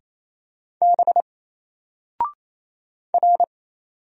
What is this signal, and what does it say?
Morse code, heard as a keyed tone, about 27 wpm, 700 Hz. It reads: B R
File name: Single-Letter-Number-28wpm